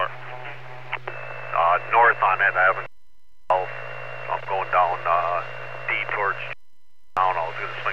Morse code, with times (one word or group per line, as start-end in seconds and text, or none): none